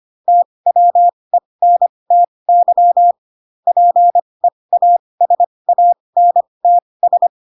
Morse code, heard as a keyed tone, about 25 wpm, spaced as for 20 wpm, 700 Hz. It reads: TWENTY PEASANTS